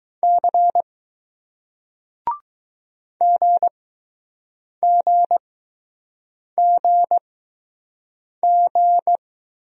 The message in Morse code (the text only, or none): C G G G G